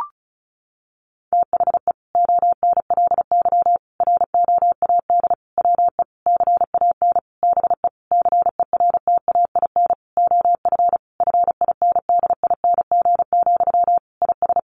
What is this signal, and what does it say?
Morse code, heard as a keyed tone, about 34 wpm, 700 Hz. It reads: THE ONLY ROAD WE CAN BE CERTAIN OF FINDING, IS
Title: Sentences from Top 1100 Words 35wpm